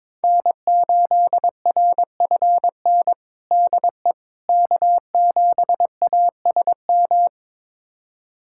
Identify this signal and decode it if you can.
Morse code, heard as a keyed tone, about 22 wpm, 700 Hz. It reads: N8RFN DE K7ASM
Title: USA Callsign DE Callsign - 5 Characters - 22wpm